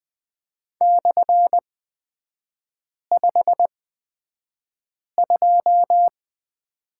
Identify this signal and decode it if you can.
Morse code, heard as a keyed tone, about 20 wpm, spaced as for 5 wpm, 690 Hz. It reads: /52
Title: 229 Lesson 37e - Letter x - Three Characters 20wpm